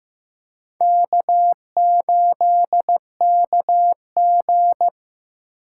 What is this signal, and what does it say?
Morse code, heard as a keyed tone, about 15 wpm, 700 Hz. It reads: K8KG
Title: USA Call Signs - 4 Characters - 3x Character Spacing - 15wpm